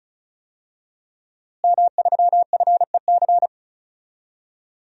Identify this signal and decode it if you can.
Morse code, heard as a keyed tone, about 36 wpm, 700 Hz. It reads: M3FEC
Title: DXCC-100 Call Signs - 5 Characters - Repeated 2x - 35wpm